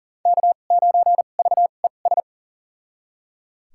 Morse code, heard as a keyed tone, about 38 wpm, spaced as for 28 wpm, 700 Hz. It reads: K9VES
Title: USA Call Signs - 5 Characters - 2x Character Spacing - 40wpm